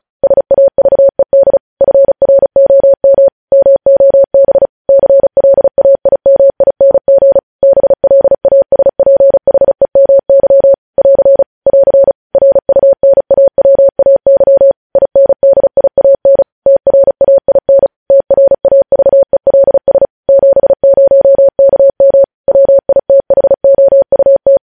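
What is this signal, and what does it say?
Morse code, heard as a keyed tone, about 36 wpm, 550 Hz. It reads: SAVED FROM MOB CLAIMING BLASPHEMY <AR> <AR> RUNAWAY INDIAN TRAIN TRAVELS 70KM WITHOUT
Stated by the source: Morse practice sending news headlines at 35 wpm.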